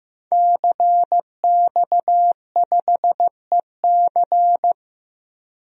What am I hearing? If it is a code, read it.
Morse code, heard as a keyed tone, about 15 wpm, 700 Hz. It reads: CX5EC
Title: DXCC-100 Call Signs - 5 Characters - Repeated 3x - 15wpm